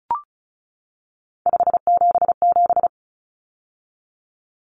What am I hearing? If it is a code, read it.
Morse code, heard as a keyed tone, about 34 wpm, 700 Hz. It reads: 577
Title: Numbers - 3-Digit 35wpm